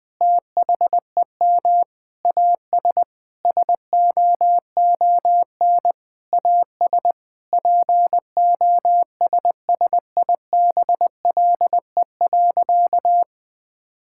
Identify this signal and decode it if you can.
Morse code, heard as a keyed tone, about 20 wpm, 700 Hz. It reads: THEM AS SOON AS POSSIBLE.
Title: Sentences from Top 1000 Words - Review Part 5 - 20wpm